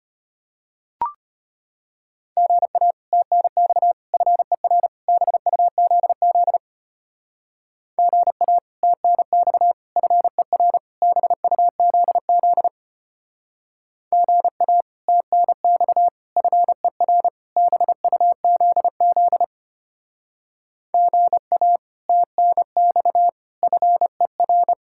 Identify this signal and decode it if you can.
Morse code, heard as a keyed tone, about 32 wpm, 700 Hz. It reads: GA TNX FER BUZZ GA TNX FER BUZZ GA TNX FER BUZZ GA TNX FER